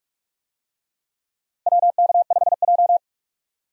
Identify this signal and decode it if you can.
Morse code, heard as a keyed tone, about 43 wpm, 700 Hz. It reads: WK5J
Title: USA Call Signs - 4 Characters - Repeated 3x - 45wpm